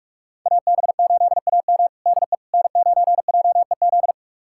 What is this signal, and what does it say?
Morse code, heard as a keyed tone, about 43 wpm, 700 Hz. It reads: AB8AM DE N9JEZ
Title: USA Callsign DE Callsign - 5 Characters - 45wpm